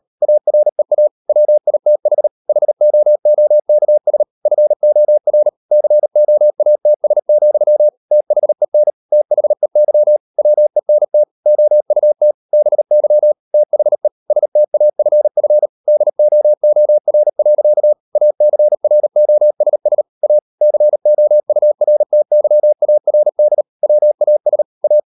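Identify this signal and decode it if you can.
Morse code, heard as a keyed tone, about 38 wpm, 600 Hz. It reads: AREA WITH HOOKS FOR COATS, THEN THEY WENT OUT BY THE STAFF DOOR. ACROSS A COURTYARD WAS A
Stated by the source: Morse code practice at 38 WPM, sending Ken Follett's novel Never in English, from CW med Gnister.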